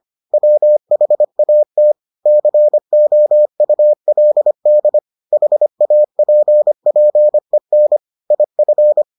Morse code, heard as a keyed tone, about 25 wpm, 600 Hz. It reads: WHAT COULD HAPPEN IF